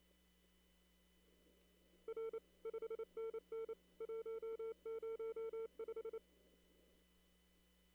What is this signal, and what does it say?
Morse code, heard as a keyed tone, about 25 wpm, 410 Hz. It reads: R 5NN 105
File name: CW / kt5j wpx cw-op-k5pi